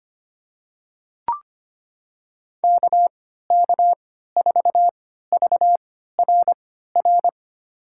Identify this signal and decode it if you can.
Morse code, heard as a keyed tone, about 25 wpm, spaced as for 14 wpm, 700 Hz. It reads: KK4VRR
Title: USA Call Signs - 3x Character Spacing - 25wpm